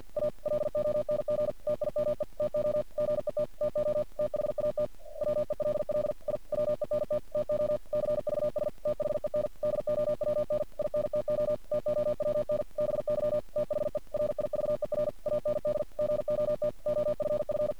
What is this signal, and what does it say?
Morse code, heard as a keyed tone, about 51 wpm, 600 Hz. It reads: A LONG TIME TO GET TO THAT WELL I WENT TO CVS THEN DOWN INTO TOWN BY THE RIVER AND GOT OFF
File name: archived / CRSnet